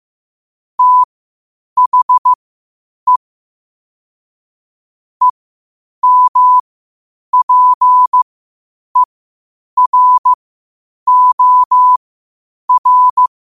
Morse code, heard as a keyed tone, about 15 wpm, spaced as for 8 wpm, 1000 Hz. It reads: THE EMPEROR
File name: qotdfiles / 8wpm